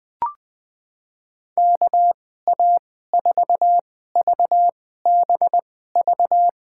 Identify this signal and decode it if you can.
Morse code, heard as a keyed tone, about 20 wpm, spaced as for 14 wpm, 700 Hz. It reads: KA4VBV
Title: USA Call Signs - 2x Character Spacing - 20wpm